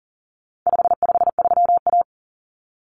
Morse code, heard as a keyed tone, about 40 wpm, 700 Hz. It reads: 553A
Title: Common Cut Numbers - 4-Digit 40wpm